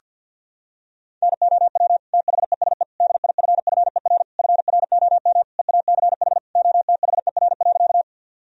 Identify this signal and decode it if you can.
Morse code, heard as a keyed tone, about 52 wpm, 700 Hz. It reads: NOW THESE DIFFER FROM EACH OTHER.
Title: Sentences from Top 1000 Words - Review Part 1 - 50wpm